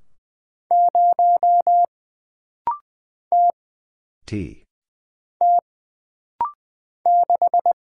Morse code, heard as a keyed tone, about 20 wpm, 700 Hz. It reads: 0 T T 6
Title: Single Letter-Number - Rapid-Fire 20wpm